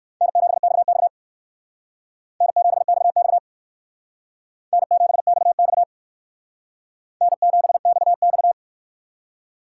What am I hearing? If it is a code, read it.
Morse code, heard as a keyed tone, about 64 wpm, 700 Hz. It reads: N7XX N7XX N7XX N7XX